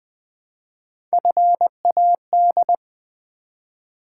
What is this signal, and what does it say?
Morse code, heard as a keyed tone, about 20 wpm, 700 Hz. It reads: FAD
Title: Build A Word - 1 Letter At A Time - 4-Letter Words - 20wpm